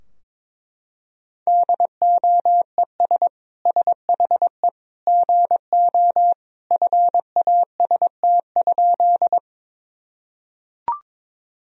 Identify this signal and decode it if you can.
Morse code, heard as a keyed tone, about 22 wpm, 700 Hz. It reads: DOES SHE GO FAST?
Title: Sentences from Top 1000 Words - Review Part 2 - 22wpm